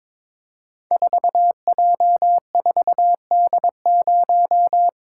Morse code, heard as a keyed tone, about 22 wpm, 700 Hz. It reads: 4J4D0
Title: Code Groups - 5 Letter:Numbers - 22wpm